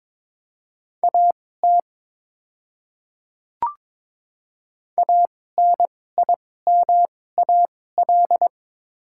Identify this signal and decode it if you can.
Morse code, heard as a keyed tone, about 22 wpm, spaced as for 16 wpm, 700 Hz. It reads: AT ANIMAL
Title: 2x Character Spacing - Top 200 Words - 22wpm